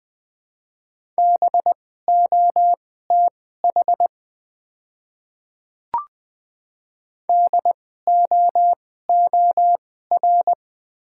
Morse code, heard as a keyed tone, about 20 wpm, spaced as for 14 wpm, 700 Hz. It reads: BOTH DOOR
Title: 2x Character Spacing - Top 300 Words - 20wpm